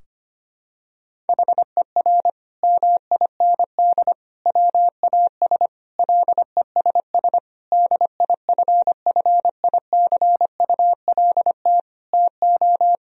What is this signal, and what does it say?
Morse code, heard as a keyed tone, about 25 wpm, 700 Hz. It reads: HER MIND WAS LESS DIFFICULT TO
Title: Sentences from Top 1000 Words - Review Part 2 - 25wpm